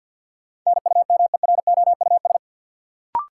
Morse code, heard as a keyed tone, about 52 wpm, 700 Hz. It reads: NUMEROUS
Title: Sentences from Top 1400 Words - New Word Drill Alt - 50wpm